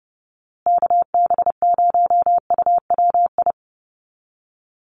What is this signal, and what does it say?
Morse code, heard as a keyed tone, about 30 wpm, 700 Hz. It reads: KB0UWI
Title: USA Call Signs Repeated 2x 30wpm